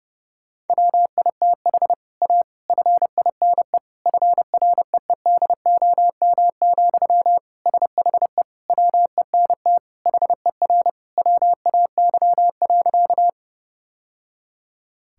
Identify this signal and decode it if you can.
Morse code, heard as a keyed tone, about 30 wpm, 700 Hz. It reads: WITH A FINE FREEDOM, SHE WENT HER WAY.
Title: Sentences from Top 1000 Words - Review Part 5 - 30wpm